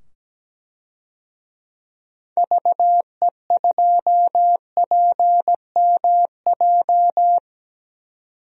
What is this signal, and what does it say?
Morse code, heard as a keyed tone, about 17 wpm, 700 Hz. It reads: VE2PMJ
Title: DXCC-100 Call Signs - 6 Characters - 17wpm